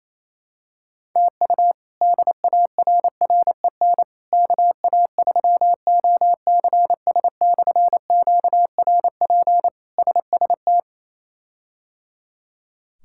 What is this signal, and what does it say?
Morse code, heard as a keyed tone, about 28 wpm, 700 Hz. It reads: TU DARREN KA3OCS/QRP SST
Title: K1USN SST QSO Practice - 28wpm